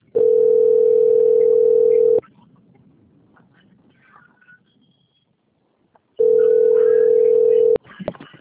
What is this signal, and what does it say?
Ringback tone (North America), 440 + 480 Hz.
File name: out- - - - - .47799